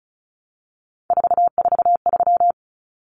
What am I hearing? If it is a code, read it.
Morse code, heard as a keyed tone, about 34 wpm, 700 Hz. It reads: 443
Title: Numbers - 3-Digit 35wpm